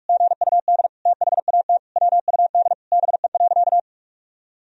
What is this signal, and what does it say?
Morse code, heard as a keyed tone, about 47 wpm, 700 Hz. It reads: GUD THAT WUD BE.
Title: Sentences from Top 200 Words and Abbreviations 45wpm